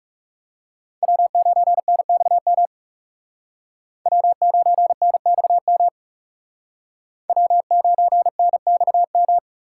Morse code, heard as a keyed tone, about 38 wpm, 700 Hz. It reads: W9NXM W9NXM W9NXM